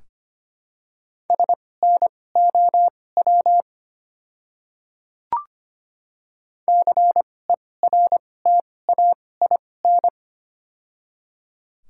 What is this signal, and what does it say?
Morse code, heard as a keyed tone, about 25 wpm, spaced as for 18 wpm, 700 Hz. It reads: SNOW CERTAIN
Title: 2x Character Spacing - Top 500 Words - 25wpm